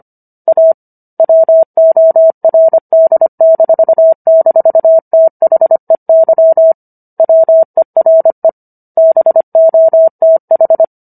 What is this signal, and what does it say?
Morse code, heard as a keyed tone, about 25 wpm, 650 Hz. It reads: A WORD--THEY WERE BOTH